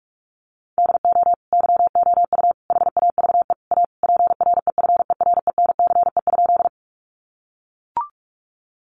Morse code, heard as a keyed tone, about 44 wpm, 700 Hz. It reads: DO YOU HAVE A PREFERENCE?
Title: Sentences from Top 1500 Words 45wpm